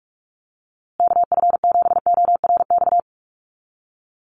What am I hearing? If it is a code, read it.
Morse code, heard as a keyed tone, about 44 wpm, 700 Hz. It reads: KF7ORX
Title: USA Call Signs Repeated 3x 45wpm